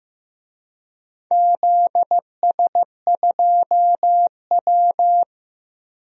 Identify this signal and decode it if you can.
Morse code, heard as a keyed tone, about 15 wpm, 700 Hz. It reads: ZS2W
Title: DXCC-100 Call Signs - 4 Characters - Repeated 2x - 15wpm